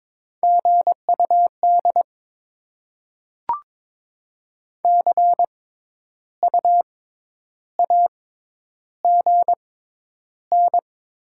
Morse code, heard as a keyed tone, about 22 wpm, spaced as for 10 wpm, 700 Hz. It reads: GUD CU AGN
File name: Top 20 QSO Elements - 6x Character Spacing - 22wpm